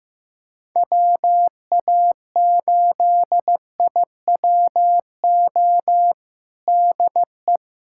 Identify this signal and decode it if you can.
Morse code, heard as a keyed tone, about 15 wpm, 700 Hz. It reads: WA8IWO DE
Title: USA Callsign DE Callsign 15wpm